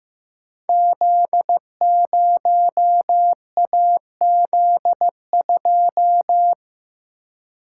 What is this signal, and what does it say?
Morse code, heard as a keyed tone, about 15 wpm, 700 Hz. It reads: Z0AZ2
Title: Code Groups - 5 Letter:Numbers - 15wpm